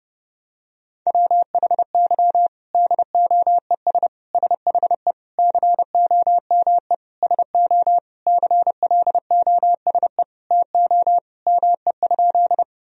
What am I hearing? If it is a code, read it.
Morse code, heard as a keyed tone, about 32 wpm, 700 Hz. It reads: WHY DOES SHE COME SO CLOSE TO ME?
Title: Sentences from Top 1000 Words - Review Part 1 - 30wpm